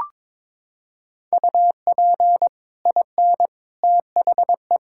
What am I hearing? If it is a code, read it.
Morse code, heard as a keyed tone, about 22 wpm, 700 Hz. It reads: UP IN THE
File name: practice / 22wpm / Sets of 7 Words - 22wpm